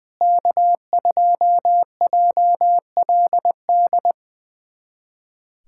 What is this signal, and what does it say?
Morse code, heard as a keyed tone, about 20 wpm, 700 Hz. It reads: K2JLD
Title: USA Callsign DE Callsign - 5 Characters - 20wpm